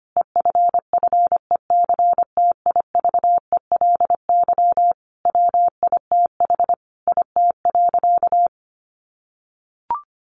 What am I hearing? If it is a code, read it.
Morse code, heard as a keyed tone, about 25 wpm, 700 Hz. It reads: EFFECTIVELY WITH IT.
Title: Sentences from Top 1400 Words 25wpm